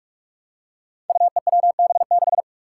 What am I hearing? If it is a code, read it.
Morse code, heard as a keyed tone, about 44 wpm, 700 Hz. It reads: UEWB6
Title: Five Challenging Letter-Number Combinations 45wpm